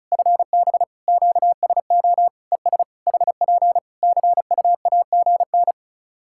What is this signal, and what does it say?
Morse code, heard as a keyed tone, about 34 wpm, 700 Hz. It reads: FB QSO ES HP CUAGN
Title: Practice QSO - Exchange 4 - Ending - 35wpm